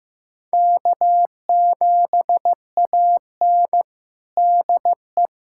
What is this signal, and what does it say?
Morse code, heard as a keyed tone, about 16 wpm, 700 Hz. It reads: K7AN DE
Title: USA Callsign DE Callsign - 4 Characters - 15wpm